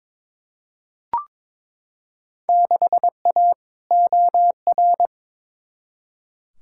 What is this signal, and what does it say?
Morse code, heard as a keyed tone, about 21 wpm, 700 Hz. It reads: 6A OR
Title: Field Day Exchanges - 22wpm